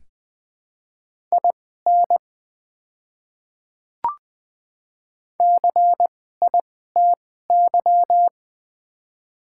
Morse code, heard as a keyed tone, about 20 wpm, spaced as for 14 wpm, 700 Hz. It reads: IN CITY
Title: 2x Character Spacing - Top 300 Words - 20wpm